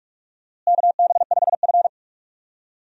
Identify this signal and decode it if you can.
Morse code, heard as a keyed tone, about 42 wpm, 700 Hz. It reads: KB5F
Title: USA Call Signs - 4 Characters - Repeated 3x - 45wpm